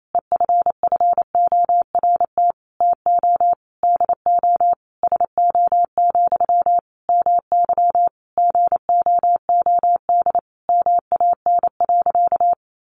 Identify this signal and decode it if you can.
Morse code, heard as a keyed tone, about 28 wpm, 700 Hz. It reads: EFFORT TO DO SO, MY GOOD MAN.
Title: Sentences from Top 1200 Words 28wpm